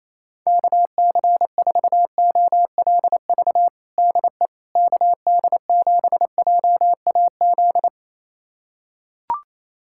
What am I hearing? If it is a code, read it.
Morse code, heard as a keyed tone, about 28 wpm, 700 Hz. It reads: KC4OLV DE KD7JAZ
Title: USA Callsign DE Callsign - 6 Characters - 28wpm